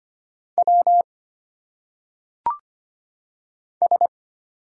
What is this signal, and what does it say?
Morse code, heard as a keyed tone, about 25 wpm, 700 Hz. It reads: W S
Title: Single Letters - Phonetic Pronunciation 25wpm